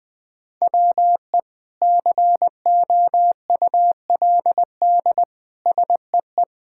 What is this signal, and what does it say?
Morse code, heard as a keyed tone, about 20 wpm, 700 Hz. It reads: WE COULD SEE
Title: Sentences from Top 1000 Words - Review Part 5 - 20wpm